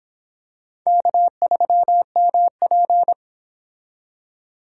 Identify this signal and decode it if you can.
Morse code, heard as a keyed tone, about 26 wpm, 700 Hz. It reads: K3MP